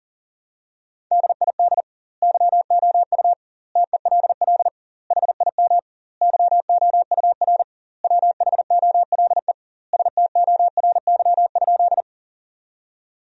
Morse code, heard as a keyed tone, about 37 wpm, 700 Hz. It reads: DID YOU TELL HIM YOUR WHOLE STORY?
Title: 2x Word-Spacing - Sentences from Top 400 Words - 40wpm